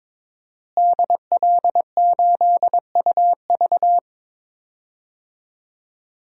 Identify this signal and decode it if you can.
Morse code, heard as a keyed tone, about 22 wpm, 700 Hz. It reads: DL8UV
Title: DXCC-100 Call Signs - 5 Characters - Repeated 2x - 22wpm